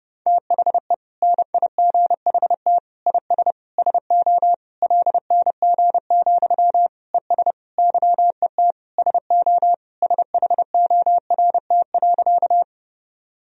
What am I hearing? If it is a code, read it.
Morse code, heard as a keyed tone, about 30 wpm, 700 Hz. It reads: THE NIGHT IS SO LNG, ES YET SO SHORT.
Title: Sentences from Top 500 Words and Abbreviations 30wpm